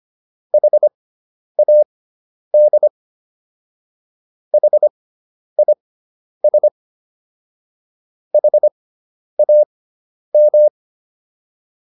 Morse code, heard as a keyed tone, about 25 wpm, spaced as for 10 wpm, 600 Hz. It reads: HAD HIS HAM